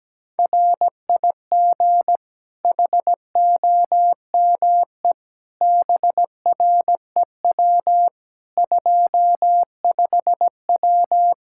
Morse code, heard as a keyed tone, about 17 wpm, 700 Hz. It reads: RIG HOME BREW 25W
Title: Practice QSO - Exchange 2 - Info - 17wpm